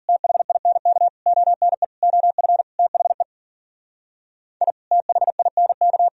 Morse code, heard as a keyed tone, about 52 wpm, 700 Hz. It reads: THINK ONE OF THE I THINK